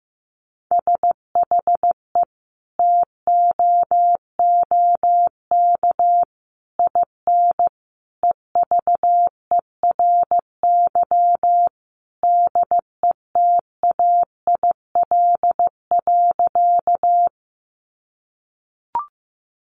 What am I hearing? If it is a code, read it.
Morse code, heard as a keyed tone, about 15 wpm, 700 Hz. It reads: SHE TOOK IN EVERY DETAIL.
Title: Sentences from Top 1500 Words 15wpm